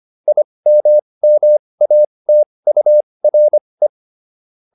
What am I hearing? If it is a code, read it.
Morse code, heard as a keyed tone, about 25 wpm, spaced as for 20 wpm, 600 Hz. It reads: IMMATURE